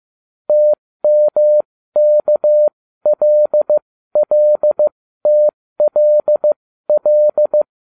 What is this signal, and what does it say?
Morse code, heard as a keyed tone, about 15 wpm, spaced as for 13 wpm, 600 Hz. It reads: TMKLLTLL